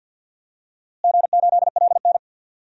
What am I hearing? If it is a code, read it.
Morse code, heard as a keyed tone, about 51 wpm, 700 Hz. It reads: G8LN
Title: DXCC-100 Call Signs - 4 Characters - Repeated 2x - 50wpm